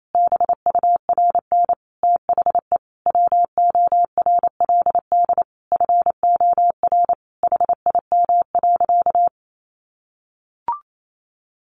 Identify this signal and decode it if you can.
Morse code, heard as a keyed tone, about 28 wpm, 700 Hz. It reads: BURN THE WORLD FOR HIM.
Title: Sentences from Top 1300 Words 28wpm